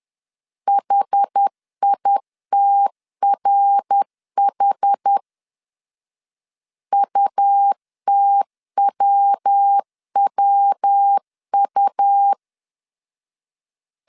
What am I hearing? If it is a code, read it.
Morse code, heard as a keyed tone, about 10 wpm, 800 Hz. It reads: HITRH UTWWU
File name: CW / DET / 1 3 1500C S B 10wpm 3335